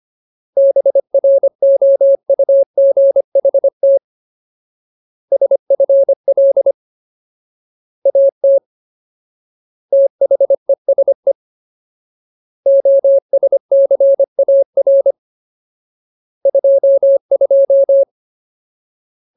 Morse code, heard as a keyed tone, about 25 wpm, 550 Hz. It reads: BROUGHT SFL AT THESE OSCAR 22
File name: Ultimate Word List / 25WPM Ultimate